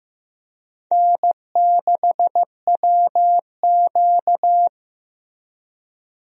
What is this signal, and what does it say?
Morse code, heard as a keyed tone, about 15 wpm, 700 Hz. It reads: N6WQ
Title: USA Call Signs - 4 Characters - Repeated 3x - 15wpm